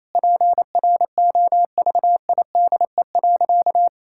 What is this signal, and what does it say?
Morse code, heard as a keyed tone, about 28 wpm, 700 Hz. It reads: PROVIDE.
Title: Sentences from Top 1000 Words - Review Part 6 - 28wpm